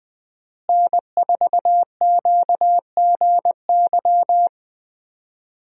Morse code, heard as a keyed tone, about 20 wpm, 700 Hz. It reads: N4QGY
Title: USA Call Signs - 5 Characters - Repeated 2x - 20wpm